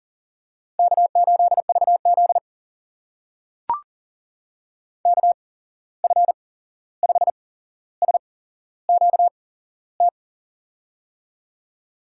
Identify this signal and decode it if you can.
Morse code, heard as a keyed tone, about 39 wpm, spaced as for 15 wpm, 700 Hz. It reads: K8VZ KF5SQT
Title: USA Call Signs - 8x Character Spacing - 40wpm